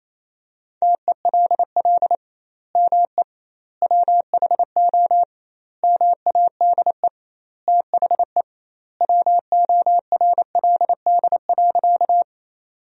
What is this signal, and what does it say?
Morse code, heard as a keyed tone, about 28 wpm, 700 Hz. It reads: TELL ME WHO MADE THE WORLD.
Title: 2x Word-Spacing - Sentences from Top 200 Words - 28wpm